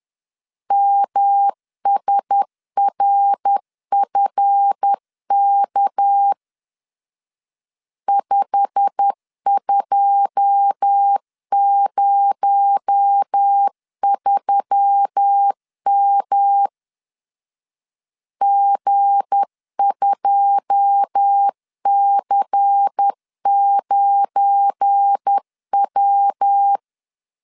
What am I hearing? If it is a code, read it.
Morse code, heard as a keyed tone, about 10 wpm, 800 Hz. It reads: MSRFK 5203M G2C9W
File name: LEZIONI / 9 2 10wpm 2736